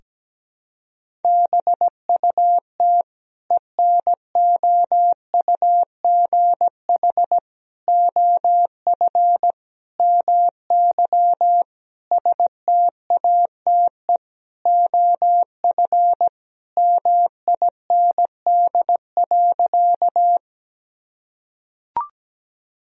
Morse code, heard as a keyed tone, about 17 wpm, 700 Hz. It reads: BUT ENOUGH OF MY STATE OF MIND.
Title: Sentences from Top 1000 Words - Review Part 2 - 17wpm